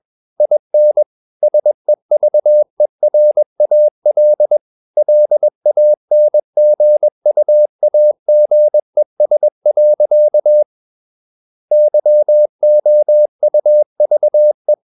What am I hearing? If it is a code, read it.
Morse code, heard as a keyed tone, about 21 wpm, 600 Hz. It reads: IN SEVERAL LANGUAGES. YOUVE
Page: CW med Gnister